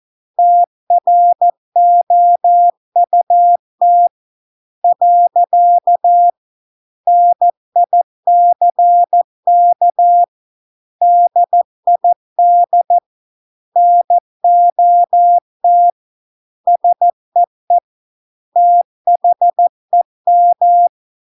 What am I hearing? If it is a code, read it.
Morse code, heard as a keyed tone, about 14 wpm, 700 Hz. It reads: TROUT . NICK DID NOT SEE THEM